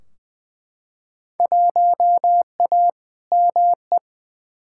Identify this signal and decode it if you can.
Morse code, heard as a keyed tone, about 20 wpm, 700 Hz. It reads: 1A ME
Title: Field Day Exchanges Encore - 20wpm